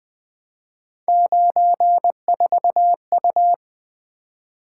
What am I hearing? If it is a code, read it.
Morse code, heard as a keyed tone, about 20 wpm, 700 Hz. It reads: 94U
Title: Three Letter-Number Combinations - 20wpm